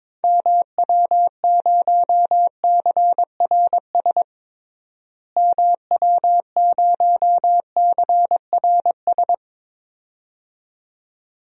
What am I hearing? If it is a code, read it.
Morse code, heard as a keyed tone, about 21 wpm, 700 Hz. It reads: MW0CRS MW0CRS
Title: DXCC-100 Call Signs - 6 Characters - Repeated 2x - 22wpm